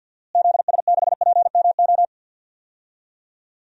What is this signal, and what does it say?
Morse code, heard as a keyed tone, about 46 wpm, 700 Hz. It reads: ZS6PMO
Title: DXCC-100 Call Signs - 6 Characters - Repeated 2x - 50wpm